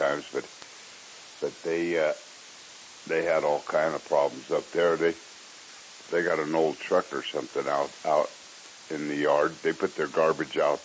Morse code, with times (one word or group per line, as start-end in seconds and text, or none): none